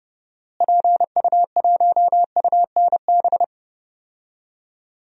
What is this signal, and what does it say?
Morse code, heard as a keyed tone, about 31 wpm, 700 Hz. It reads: PU1UNB
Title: DXCC-100 Call Signs - 6 Characters - Repeated 2x - 30wpm